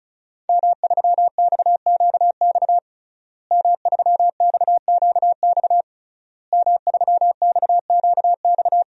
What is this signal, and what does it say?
Morse code, heard as a keyed tone, about 36 wpm, 700 Hz. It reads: M3XQX M3XQX M3XQX
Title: DXCC-100 Call Signs - 5 Characters - Repeated 3x - 35wpm